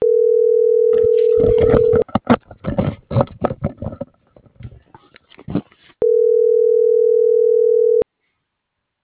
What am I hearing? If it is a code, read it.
Ringback tone (North America), 440 + 480 Hz.